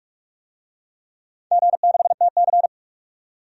Morse code, heard as a keyed tone, about 46 wpm, 700 Hz. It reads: G6TC
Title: DXCC-100 Call Signs - 4 Characters - 45wpm